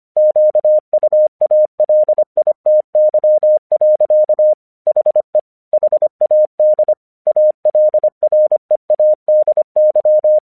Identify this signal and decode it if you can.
Morse code, heard as a keyed tone, about 25 wpm, 610 Hz. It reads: QUALITY. HE HAD ALREADY
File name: 1 English 25wpm / AQC26